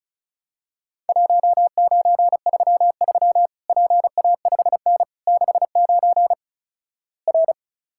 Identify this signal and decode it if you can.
Morse code, heard as a keyed tone, about 35 wpm, 700 Hz. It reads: 1933 PA5N 69 R
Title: Worked All Europe DX Contest - QTC Exchange - 35wpm